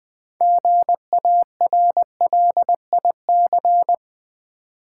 Morse code, heard as a keyed tone, about 20 wpm, 690 Hz. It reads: GARLIC
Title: Difficult Words with Hints - 20wpm